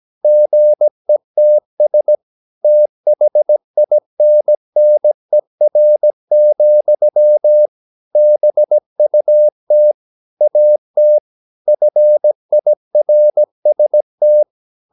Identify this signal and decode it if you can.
Morse code, heard as a keyed tone, about 17 wpm, 600 Hz. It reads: GETS THINNER, BUT AT FIRST
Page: Romanen Never af Ken Follet på engelsk.